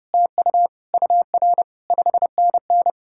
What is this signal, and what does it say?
Morse code, heard as a keyed tone, about 37 wpm, 700 Hz. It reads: TU UR 5NN
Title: Speed-Racing Common QSO Phrases Encore - 30wpm